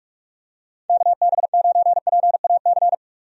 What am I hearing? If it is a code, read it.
Morse code, heard as a keyed tone, about 43 wpm, 700 Hz. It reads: KB9PAC
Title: USA Call Signs - 6 Characters - Repeated 2x - 45wpm